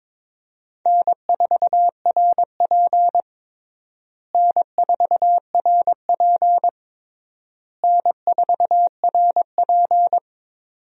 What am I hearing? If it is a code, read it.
Morse code, heard as a keyed tone, about 22 wpm, 700 Hz. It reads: N4RP N4RP N4RP
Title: USA Call Signs - 4 Characters - Repeated 3x - 22wpm